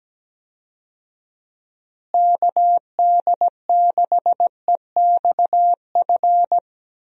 Morse code, heard as a keyed tone, about 17 wpm, 700 Hz. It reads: KD6EXF
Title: USA Call Signs - 6 Characters - Repeated 3x - 17wpm